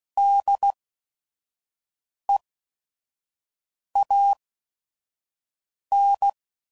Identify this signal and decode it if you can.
Morse code, heard as a keyed tone, about 16 wpm, spaced as for 5 wpm, 800 Hz. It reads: DEAN
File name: Prefixes / By First Letter / Ds